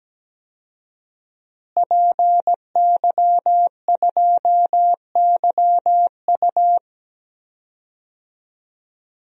Morse code, heard as a keyed tone, about 17 wpm, 700 Hz. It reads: PY2YU
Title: DXCC-100 Call Signs - 5 Characters - Repeated 3x - 17wpm